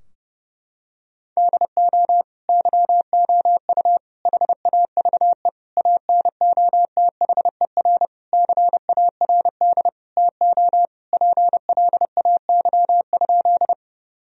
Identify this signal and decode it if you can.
Morse code, heard as a keyed tone, about 30 wpm, 700 Hz. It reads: DO YOU HAVE ANOTHER CARD TO PLAY?
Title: Sentences from Top 1000 Words - Review Part 4 - 30wpm